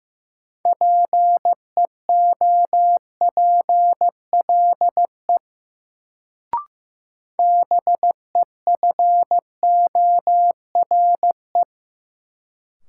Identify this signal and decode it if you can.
Morse code, heard as a keyed tone, about 15 wpm, 700 Hz. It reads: PEOPLE BEFORE
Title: Sentences from Top 200 Words - New Word Drill - 15wpm